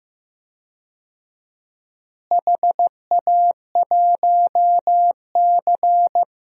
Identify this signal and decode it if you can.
Morse code, heard as a keyed tone, about 15 wpm, 700 Hz. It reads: HA1C
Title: DXCC-100 Call Signs - 4 Characters - Repeated 3x - 15wpm